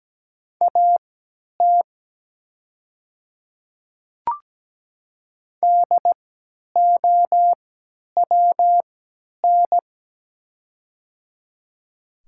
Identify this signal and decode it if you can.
Morse code, heard as a keyed tone, about 17 wpm, spaced as for 10 wpm, 700 Hz. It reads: AT DOWN
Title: 3x Character Spacing - Top 100 Words - 17wpm